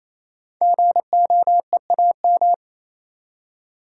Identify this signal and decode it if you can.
Morse code, heard as a keyed tone, about 29 wpm, 700 Hz. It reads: GOEAM
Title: Code Groups - 5 Letters - 28wpm